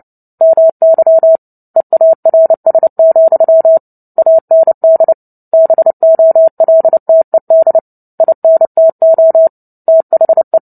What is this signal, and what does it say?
Morse code, heard as a keyed tone, about 30 wpm, 650 Hz. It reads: MY EARS, AND BOLTED INTO THE
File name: Book Two / 03-Chapter 3 - 29 WPM